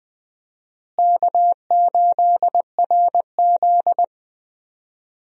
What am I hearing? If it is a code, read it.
Morse code, heard as a keyed tone, about 20 wpm, 700 Hz. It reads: K8RZ
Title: USA Call Signs - 4 Characters - Repeated 2x - 20wpm